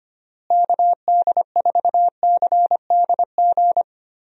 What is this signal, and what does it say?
Morse code, heard as a keyed tone, about 25 wpm, 700 Hz. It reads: KD4CDG
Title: USA Call Signs - 6 Characters - Repeated 2x - 25wpm